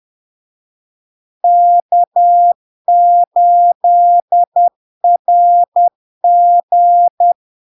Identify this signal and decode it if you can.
Morse code, heard as a keyed tone, about 10 wpm, 700 Hz. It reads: K8RG
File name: LARK-FD-2022-Log-in-CW-10wpm-700-Hz